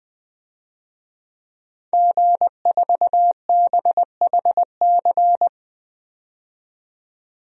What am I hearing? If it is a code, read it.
Morse code, heard as a keyed tone, about 20 wpm, 700 Hz. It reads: G4BHC
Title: DXCC-100 Call Signs - 5 Characters - 20wpm